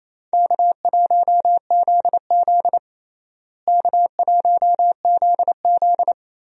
Morse code, heard as a keyed tone, about 28 wpm, 700 Hz. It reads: K1ZZ K1ZZ
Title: USA Call Signs - 4 Characters - Repeated 3x - 28wpm